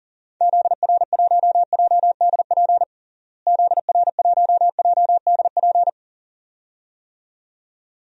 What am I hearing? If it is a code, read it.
Morse code, heard as a keyed tone, about 40 wpm, 700 Hz. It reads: ZR1JDP ZR1JDP
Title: DXCC-100 Call Signs - 6 Characters - Repeated 2x - 40wpm